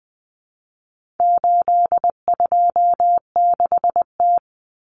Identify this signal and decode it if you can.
Morse code, heard as a keyed tone, about 20 wpm, 700 Hz. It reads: 826T
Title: Common Cut Numbers - 4-Digit 20wpm